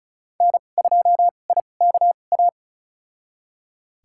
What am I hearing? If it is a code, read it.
Morse code, heard as a keyed tone, about 34 wpm, spaced as for 25 wpm, 700 Hz. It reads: N2IKA
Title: USA Call Signs - 5 Characters - 2x Character Spacing - 35wpm